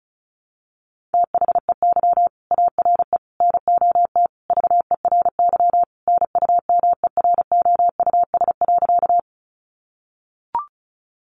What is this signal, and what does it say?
Morse code, heard as a keyed tone, about 34 wpm, 700 Hz. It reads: THEY ARE NOT VERY NUMEROUS.
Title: Sentences from Top 1400 Words 35wpm